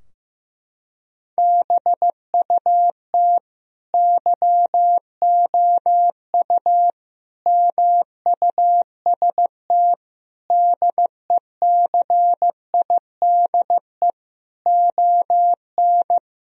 Morse code, heard as a keyed tone, about 16 wpm, 700 Hz. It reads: BUT YOU MUST DECIDE ON
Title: Sentences from Top 1000 Words - Review Part 3 - 15wpm